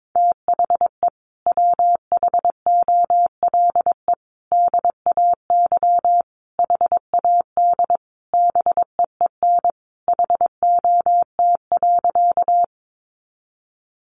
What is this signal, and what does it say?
Morse code, heard as a keyed tone, about 22 wpm, 700 Hz. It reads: THE WHOLE DAY HAD BEEN HOT.
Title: Sentences from Top 500 Words 22wpm